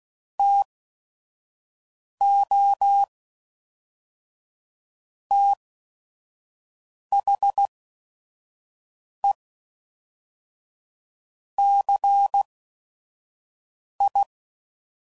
Morse code, heard as a keyed tone, about 16 wpm, spaced as for 5 wpm, 800 Hz. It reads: TO THE CI
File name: CW Code Groups @5WPM / Technical Subjects / XC